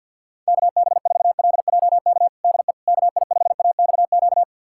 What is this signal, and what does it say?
Morse code, heard as a keyed tone, about 53 wpm, 700 Hz. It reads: KB4LJK DE KI5AXQ
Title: USA Callsign DE Callsign - 6 Characters - 50wpm